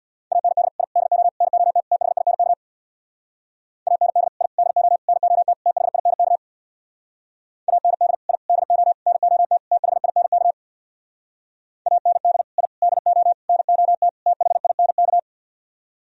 Morse code, heard as a keyed tone, about 64 wpm, 700 Hz. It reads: AND I DO NOT THINK AND I DO NOT THINK AND I DO NOT THINK AND I DO NOT THINK